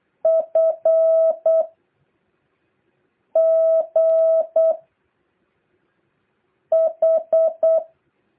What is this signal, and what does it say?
Morse code, heard as a keyed tone, about 8 wpm, 660 Hz. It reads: F G H
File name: jh8cbh / 2023cw